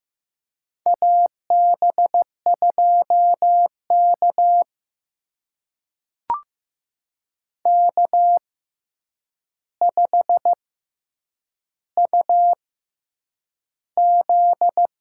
Morse code, heard as a keyed tone, about 15 wpm, spaced as for 8 wpm, 700 Hz. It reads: AB2K K5UZ
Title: USA Call Signs - 4 Characters - 6x Character Spacing - 15wpm